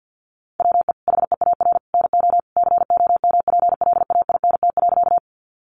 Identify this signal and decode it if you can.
Morse code, heard as a keyed tone, about 50 wpm, 700 Hz. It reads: WE HEAR NO COMPLAINT.
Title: Sentences from Top 1400 Words 50wpm